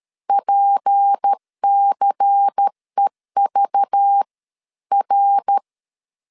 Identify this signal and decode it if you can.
Morse code, heard as a keyed tone, about 13 wpm, 800 Hz. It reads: PCEV R
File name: CW / DET / 1 9 1500C S B 12wpm 3819